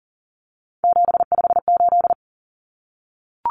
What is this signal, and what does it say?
Morse code, heard as a keyed tone, about 40 wpm, 700 Hz. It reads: 758
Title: Common Cut Numbers - 3-Digit 40wpm